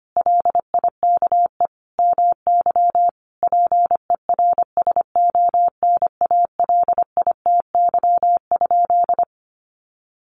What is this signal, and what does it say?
Morse code, heard as a keyed tone, about 25 wpm, 700 Hz. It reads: LIKE MY PERSONALITY?
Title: Sentences from Top 1400 Words 25wpm